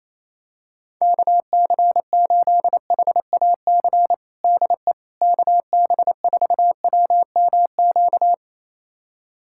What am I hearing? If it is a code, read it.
Morse code, heard as a keyed tone, about 28 wpm, 700 Hz. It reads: KC8HAC DE KB4WMQ
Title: USA Callsign DE Callsign - 6 Characters - 28wpm